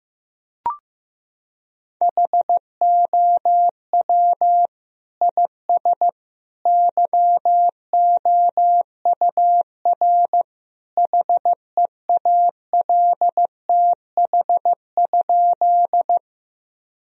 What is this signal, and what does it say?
Morse code, heard as a keyed tone, about 15 wpm, 700 Hz. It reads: HOW IS YOUR HEALTH?
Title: Sentences from Top 1000 Words - Review Part 3 - 15wpm